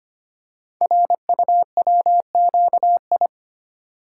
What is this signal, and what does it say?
Morse code, heard as a keyed tone, about 25 wpm, 700 Hz. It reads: RUWQI
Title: Code Groups - 5 Letters - 25wpm